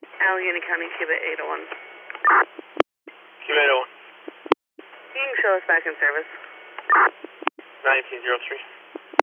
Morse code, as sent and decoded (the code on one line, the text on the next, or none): none